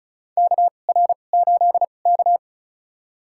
Morse code, heard as a keyed tone, about 36 wpm, spaced as for 25 wpm, 700 Hz. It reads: KR8K
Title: USA Call Signs - 4 Characters - 2x Character Spacing - 35wpm